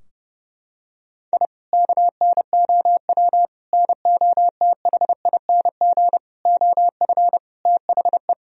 Morse code, heard as a keyed tone, about 30 wpm, 700 Hz. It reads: I KNOW NOTHING OF THE
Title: Sentences from Top 1000 Words - Review Part 3 - 30wpm